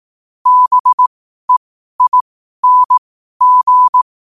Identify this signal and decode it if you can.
Morse code, heard as a keyed tone, about 18 wpm, spaced as for 12 wpm, 1000 Hz. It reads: BEING
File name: qotdfiles / 12wpm